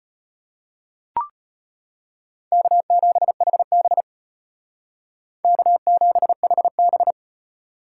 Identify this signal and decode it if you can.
Morse code, heard as a keyed tone, about 32 wpm, 700 Hz. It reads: K7HB K7HB